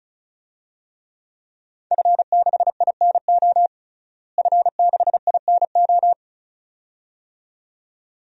Morse code, heard as a keyed tone, about 36 wpm, 700 Hz. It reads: F6INO F6INO
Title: DXCC-100 Call Signs - 5 Characters - Repeated 2x - 35wpm